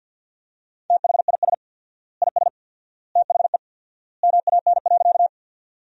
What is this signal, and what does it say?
Morse code, heard as a keyed tone, about 50 wpm, 700 Hz. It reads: THIS IS THE MAN.
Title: 4x Word-Spacing - Sentences from Top 200 Words - 50wpm